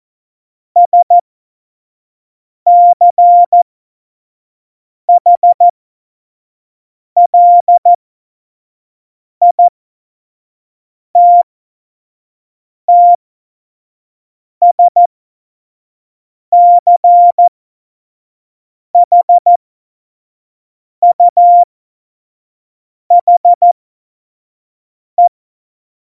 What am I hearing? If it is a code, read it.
Morse code, heard as a keyed tone, about 14 wpm, spaced as for 5 wpm, 700 Hz. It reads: SCHLITTSCHUHE
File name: Morsekurs / Witze / Witz 26. Januar 2021, 25 F70